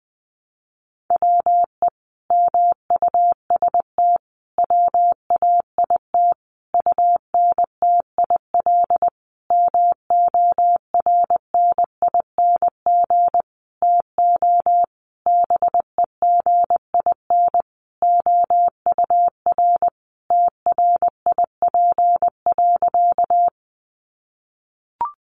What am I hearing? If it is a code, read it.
Morse code, heard as a keyed tone, about 20 wpm, 700 Hz. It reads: WE MUST WAIT UNTIL MORNING TO BEGIN OUR TRIP.
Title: Sentences from Top 1300 Words 20wpm